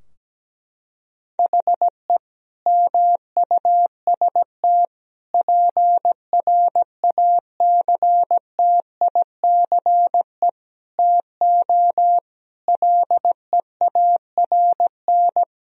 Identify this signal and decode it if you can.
Morse code, heard as a keyed tone, about 17 wpm, 700 Hz. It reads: HE MUST PRACTICE TO LEARN
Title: Sentences from Top 1000 Words - Review Part 4 - 17wpm